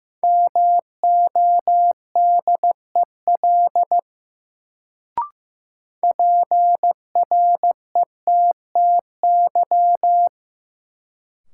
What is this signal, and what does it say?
Morse code, heard as a keyed tone, about 15 wpm, 700 Hz. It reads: MODEL PRETTY
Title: Sentences from Top 1000 Words - New Word Drill - 15wpm